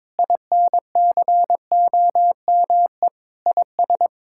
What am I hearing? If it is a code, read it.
Morse code, heard as a keyed tone, about 22 wpm, 700 Hz. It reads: INCOME IS
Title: Sentences from Top 1000 Words - Review Part 6 - 22wpm